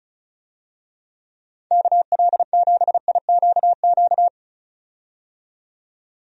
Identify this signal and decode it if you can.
Morse code, heard as a keyed tone, about 36 wpm, 690 Hz. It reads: KL7IQQ
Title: DXCC-100 Call Signs - 6 Characters - 35wpm